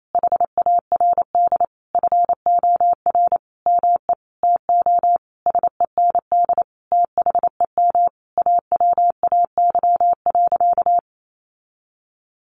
Morse code, heard as a keyed tone, about 28 wpm, 700 Hz. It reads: HARD FOR ME TO SEND THEM AWAY.
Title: Sentences from Top 1500 Words 28wpm